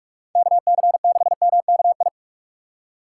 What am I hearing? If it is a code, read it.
Morse code, heard as a keyed tone, about 46 wpm, 700 Hz. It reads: KC6MKI
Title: USA Call Signs - 6 Characters - 6x Character Spacing - 45wpm